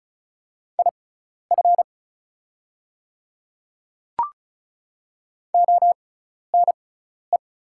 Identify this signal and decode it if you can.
Morse code, heard as a keyed tone, about 34 wpm, spaced as for 12 wpm, 690 Hz. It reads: IF ONE
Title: 6x Character Spacing - Top 500 Words - 35wpm